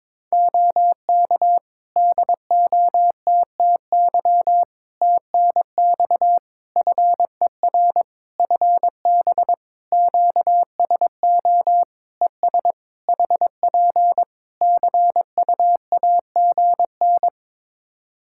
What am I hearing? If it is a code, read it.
Morse code, heard as a keyed tone, about 22 wpm, 700 Hz. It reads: OK DOTTY TNX FER FB QSO ES HP CUAGN
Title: K1USN SST QSO Practice 22wpm